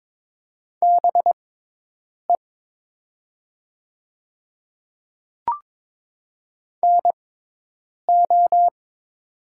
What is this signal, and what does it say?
Morse code, heard as a keyed tone, about 21 wpm, spaced as for 8 wpm, 700 Hz. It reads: BE NO
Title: 6x Character Spacing - Top 100 Words - 22wpm